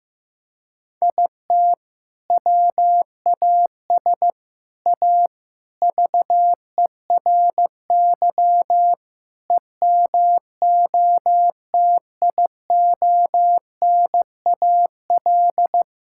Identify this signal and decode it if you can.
Morse code, heard as a keyed tone, about 15 wpm, 700 Hz. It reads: IT WAS A VERY EMOTIONAL
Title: Sentences from Top 1000 Words - Review Part 5 - 15wpm